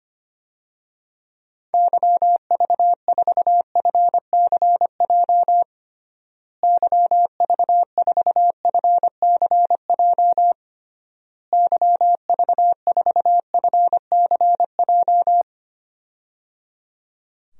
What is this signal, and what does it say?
Morse code, heard as a keyed tone, about 25 wpm, 700 Hz. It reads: YV4FCJ YV4FCJ YV4FCJ
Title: DXCC-100 Call Signs - 6 Characters - Repeated 3x - 25wpm